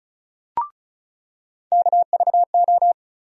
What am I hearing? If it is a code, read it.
Morse code, heard as a keyed tone, about 36 wpm, 700 Hz. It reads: KVO
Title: USA Call Sign Postfixes - 3 Letters 35wpm